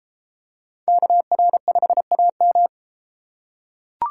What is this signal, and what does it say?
Morse code, heard as a keyed tone, about 29 wpm, 700 Hz. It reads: KR5AM